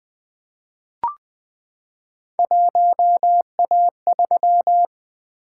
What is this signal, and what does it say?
Morse code, heard as a keyed tone, about 20 wpm, 700 Hz. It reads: 1A3
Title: Three Letter-Number Combinations - 20wpm